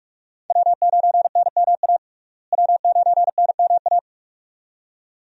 Morse code, heard as a keyed tone, about 46 wpm, 700 Hz. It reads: W9NMA W9NMA
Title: USA Call Signs - 5 Characters - Repeated 2x - 45wpm